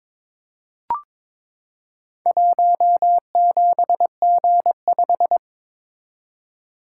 Morse code, heard as a keyed tone, about 22 wpm, 700 Hz. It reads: 17G5
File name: practice / 22wpm / Four Letter-Number Combinations - 22wpm